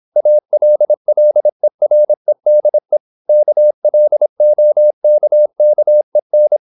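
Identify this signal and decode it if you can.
Morse code, heard as a keyed tone, about 26 wpm, 600 Hz. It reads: ALLEREDE KLOKKEN
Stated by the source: High-speed (QRQ) Morse at 26 wpm.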